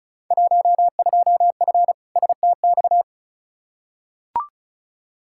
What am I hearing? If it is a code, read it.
Morse code, heard as a keyed tone, about 36 wpm, 700 Hz. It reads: 12F STX
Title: Field Day Exchanges Encore - 35wpm